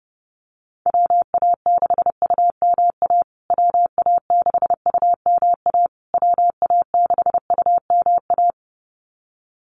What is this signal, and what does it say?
Morse code, heard as a keyed tone, about 30 wpm, 700 Hz. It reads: WA6UMA WA6UMA WA6UMA
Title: USA Call Signs Repeated 3x 30wpm